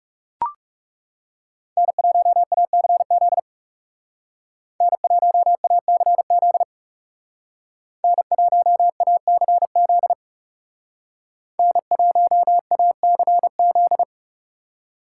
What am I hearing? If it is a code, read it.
Morse code, heard as a keyed tone, about 37 wpm, 700 Hz. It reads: N1ACZ N1ACZ N1ACZ N1ACZ
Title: Speed Racing - USA 1x3 Call Signs 30wpm